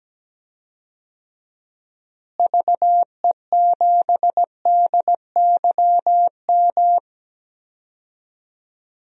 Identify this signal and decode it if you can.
Morse code, heard as a keyed tone, about 17 wpm, 700 Hz. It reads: VE7DYM
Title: DXCC-100 Call Signs - 6 Characters - Repeated 2x - 17wpm